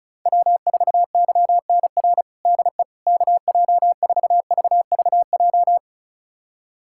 Morse code, heard as a keyed tone, about 36 wpm, 700 Hz. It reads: W4YNR DE KJ4VVJ
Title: USA Callsign DE Callsign 35wpm